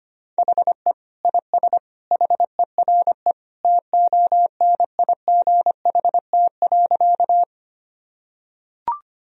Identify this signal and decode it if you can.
Morse code, heard as a keyed tone, about 25 wpm, 700 Hz. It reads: HE IS HERE TONIGHT.
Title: Sentences from Top 1000 Words - Review Part 3 - 25wpm